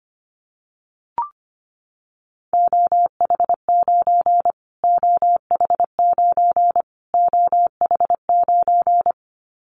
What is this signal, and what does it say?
Morse code, heard as a keyed tone, about 25 wpm, 700 Hz. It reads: OH9 OH9 OH9
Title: Callsign Prefixes for Top 25 DXCC - Repeated 3x - 25wpm